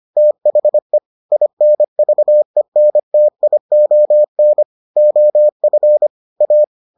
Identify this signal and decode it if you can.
Morse code, heard as a keyed tone, about 25 wpm, 600 Hz. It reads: THE INVENTION OF A